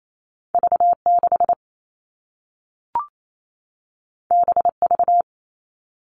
Morse code, heard as a keyed tone, about 28 wpm, 700 Hz. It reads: V6 BV
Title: All Cut Numbers - 2-Digit 28wpm